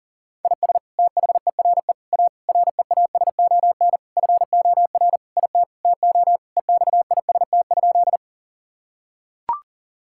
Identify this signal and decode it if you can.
Morse code, heard as a keyed tone, about 40 wpm, 700 Hz. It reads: IS THERE A REASON FOR IT TO EXIST?
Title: Sentences from Top 1000 Words - Review Part 5 - 40wpm